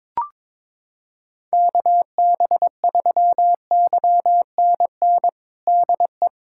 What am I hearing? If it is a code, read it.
Morse code, heard as a keyed tone, about 22 wpm, 700 Hz. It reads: KB3YNN DE
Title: USA Callsign DE Callsign - 6 Characters - 22wpm